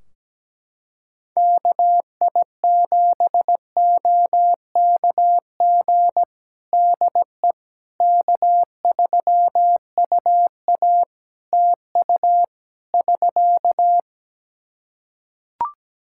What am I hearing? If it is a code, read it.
Morse code, heard as a keyed tone, about 17 wpm, 700 Hz. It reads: KI7OKG DE K3UA TU <SK>
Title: Practice QSO - Exchange 4 - Ending - 17wpm